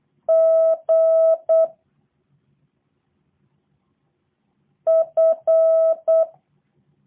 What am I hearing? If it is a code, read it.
Morse code, heard as a keyed tone, about 8 wpm, 650 Hz. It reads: G F
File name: jh8cbh / 2023cw / review1-4